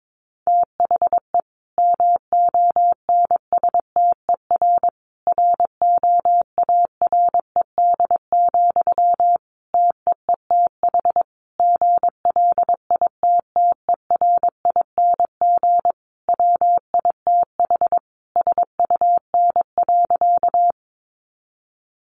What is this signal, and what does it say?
Morse code, heard as a keyed tone, about 22 wpm, 700 Hz. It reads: THE MONSTER ROARED, TEETH GLITTERING WITH SUN.
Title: A Sound Of Thunder 22wpm